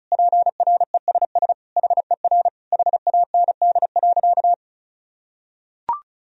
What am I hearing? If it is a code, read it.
Morse code, heard as a keyed tone, about 34 wpm, 700 Hz. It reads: PRESS HER HAND.
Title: Sentences from Top 1000 Words - Review Part 1 - 35wpm